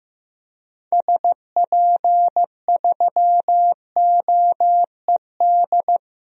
Morse code, heard as a keyed tone, about 15 wpm, 700 Hz. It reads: SP3OED
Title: DXCC-100 Call Signs - 6 Characters - Repeated 3x - 15wpm